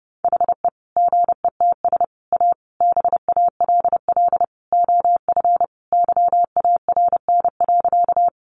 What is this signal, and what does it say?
Morse code, heard as a keyed tone, about 30 wpm, 700 Hz. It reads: HE GETS A BALL OF YARN.
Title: Additional Easy Phrases 30wpm